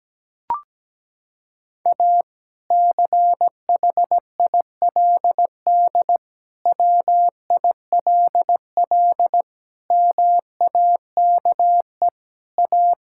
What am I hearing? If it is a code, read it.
Morse code, heard as a keyed tone, about 17 wpm, 700 Hz. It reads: A CHILD WILL MAKE A
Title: Sentences from Top 1000 Words - Review Part 6 - 17wpm